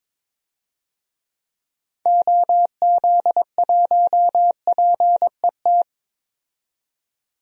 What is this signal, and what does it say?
Morse code, heard as a keyed tone, about 22 wpm, 700 Hz. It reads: OZ1PET
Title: DXCC-100 Call Signs - 6 Characters - 22wpm